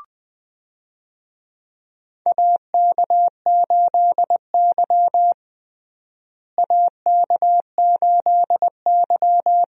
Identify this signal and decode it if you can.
Morse code, heard as a keyed tone, about 20 wpm, 700 Hz. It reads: AK8Y AK8Y
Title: USA Call Signs - 4 Characters - Repeated 2x - 20wpm